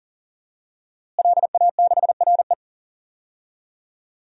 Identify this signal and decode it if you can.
Morse code, heard as a keyed tone, about 40 wpm, 700 Hz. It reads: LA6RE
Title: DXCC-100 Call Signs - 5 Characters - Repeated 3x - 40wpm